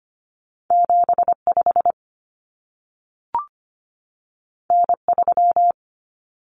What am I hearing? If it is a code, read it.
Morse code, heard as a keyed tone, about 25 wpm, 700 Hz. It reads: 75 N3
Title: Common Cut Numbers - 2-Digit 25wpm